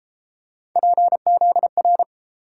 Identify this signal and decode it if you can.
Morse code, heard as a keyed tone, about 33 wpm, 700 Hz. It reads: PZR